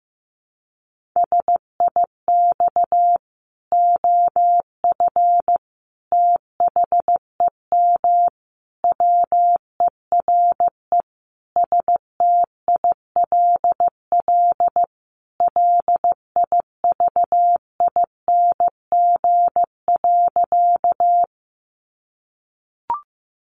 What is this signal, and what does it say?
Morse code, heard as a keyed tone, about 14 wpm, 700 Hz. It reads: SIX OF THEM WERE STILL LIVING.
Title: Sentences from Top 1100 Words 15wpm